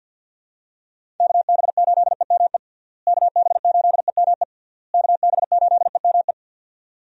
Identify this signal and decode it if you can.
Morse code, heard as a keyed tone, about 52 wpm, 700 Hz. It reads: KB8EME KB8EME KB8EME
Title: USA Call Signs - 6 Characters - Repeated 3x - 50wpm